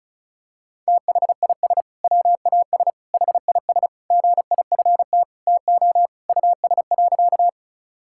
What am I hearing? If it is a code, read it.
Morse code, heard as a keyed tone, about 36 wpm, 700 Hz. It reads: THIS WAS HIS GIFT TO US.
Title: Sentences from Top 1000 Words - Review Part 6 - 35wpm